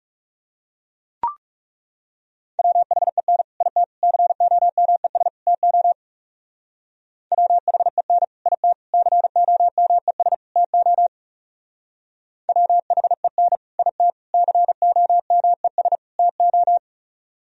Morse code, heard as a keyed tone, about 37 wpm, 700 Hz. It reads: WHEN IT COMES TO WHEN IT COMES TO WHEN IT COMES TO
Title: Speed-Racing - Sets of 4 Words 30wpm